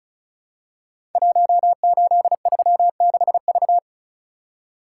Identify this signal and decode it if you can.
Morse code, heard as a keyed tone, about 36 wpm, 690 Hz. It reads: 1836V
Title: Code Groups - 5 Letter:Numbers - 35wpm